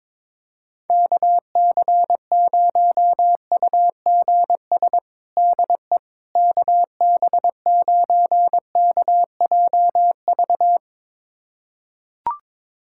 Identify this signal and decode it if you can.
Morse code, heard as a keyed tone, about 22 wpm, 700 Hz. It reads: KC0UGS DE KB9KJV
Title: USA Callsign DE Callsign - 6 Characters - 22wpm